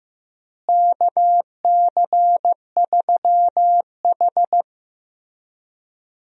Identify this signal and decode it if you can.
Morse code, heard as a keyed tone, about 16 wpm, 700 Hz. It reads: KC3H
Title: USA Call Signs - 4 Characters - Repeated 2x - 15wpm